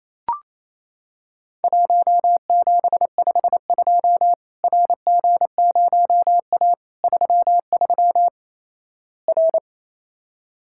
Morse code, heard as a keyed tone, about 28 wpm, 700 Hz. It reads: 1752 RG0A 33 R
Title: Worked All Europe DX Contest - QTC Exchange - 28wpm